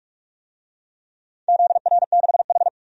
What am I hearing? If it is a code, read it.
Morse code, heard as a keyed tone, about 46 wpm, 700 Hz. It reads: ZR6H
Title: DXCC-100 Call Signs - 4 Characters - 45wpm